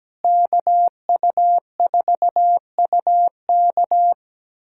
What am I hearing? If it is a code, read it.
Morse code, heard as a keyed tone, about 17 wpm, 700 Hz. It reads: KU4UK
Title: K1USN SST QSO Practice - 17wpm